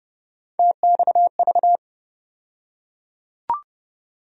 Morse code, heard as a keyed tone, about 30 wpm, 700 Hz. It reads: TXV
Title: USA Call Sign Postfixes - 3 Letters 30wpm